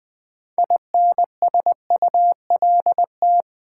Morse code, heard as a keyed tone, about 21 wpm, 700 Hz. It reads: INSULT
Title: 6 Letter Words - First 3 Letters Given - 20wpm